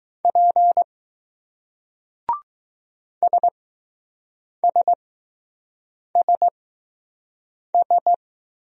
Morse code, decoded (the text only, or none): P SSSS